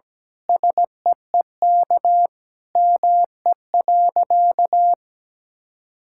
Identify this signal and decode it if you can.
Morse code, heard as a keyed tone, about 17 wpm, 700 Hz. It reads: SEEK ME.
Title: Sentences from Top 1600 Words 17wpm